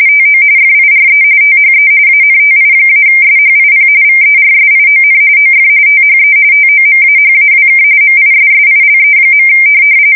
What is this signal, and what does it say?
RTTY, about 45.45 baud. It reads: TO FIRE THE ANTI-SHIP VERSION OF THE TOMAHAWK MISSILE.
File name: fsk170 / USS-New-Jersey